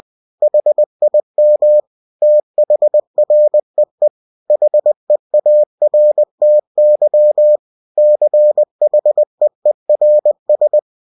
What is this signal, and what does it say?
Morse code, heard as a keyed tone, about 20 wpm, 600 Hz. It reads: HIM THREE HEARTY CHEERS
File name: cw-books / winnie-the-pooh / winnie-20wpm-page-0104